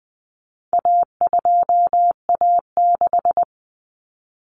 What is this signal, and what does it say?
Morse code, heard as a keyed tone, about 20 wpm, 700 Hz. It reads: A2A6
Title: Common Cut Numbers - 4-Digit 20wpm